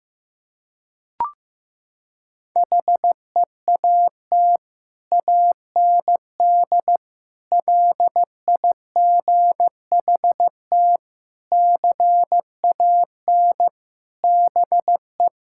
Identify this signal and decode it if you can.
Morse code, heard as a keyed tone, about 15 wpm, 700 Hz. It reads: HEAT AND LIGHT CAN BE
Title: Sentences from Top 1000 Words - Review Part 6 - 15wpm